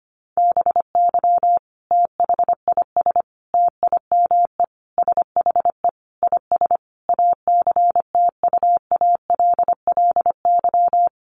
Morse code, heard as a keyed tone, about 25 wpm, 700 Hz. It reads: BY THIS TIME SHE IS ACTUALLY
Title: Sentences from Top 700 Words 25wpm